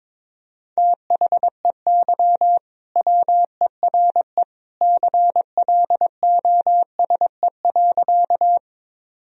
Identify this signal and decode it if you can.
Morse code, heard as a keyed tone, about 22 wpm, 700 Hz. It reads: THEY WERE CLOSE.
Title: Sentences from Top 1000 Words - Review Part 1 - 22wpm